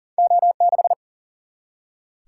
Morse code, heard as a keyed tone, about 42 wpm, 700 Hz. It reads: O6
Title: Single-Letter Single-Number 40wpm